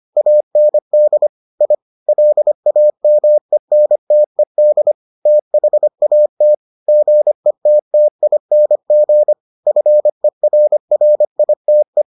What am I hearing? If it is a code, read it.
Morse code, heard as a keyed tone, about 25 wpm, 600 Hz. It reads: AND I LAMENTED THAT GETTING FERRITE